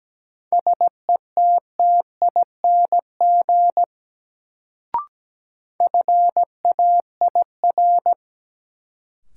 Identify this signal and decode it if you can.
Morse code, heard as a keyed tone, about 17 wpm, 700 Hz. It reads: SETTING FAIR
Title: Sentences from Top 1300 Words - New Word Drill Alt - 17wpm